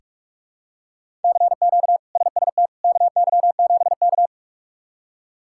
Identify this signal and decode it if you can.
Morse code, heard as a keyed tone, about 46 wpm, 700 Hz. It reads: CQ SST KY7K
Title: K1USN SST QSO Practice - 45wpm